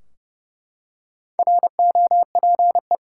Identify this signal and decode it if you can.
Morse code, heard as a keyed tone, about 30 wpm, 700 Hz. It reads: ROPE
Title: Four Letter Words - 30wpm